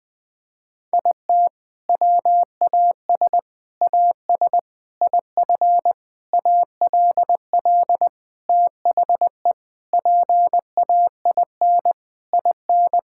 Morse code, heard as a keyed tone, about 20 wpm, 700 Hz. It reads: IT WAS AS IF ALL THE PAIN IN
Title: Sentences from Top 1600 Words 20wpm